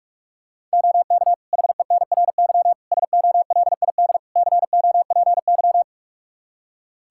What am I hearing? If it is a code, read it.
Morse code, heard as a keyed tone, about 46 wpm, 700 Hz. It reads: OK HENRY SOLID COPY
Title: Practice QSO - Exchange 3 - More Info - 45wpm